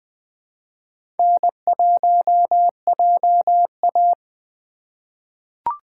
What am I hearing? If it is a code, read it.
Morse code, heard as a keyed tone, about 20 wpm, 700 Hz. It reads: N1JA
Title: USA Call Signs - 4 Characters - 3x Character Spacing - 20wpm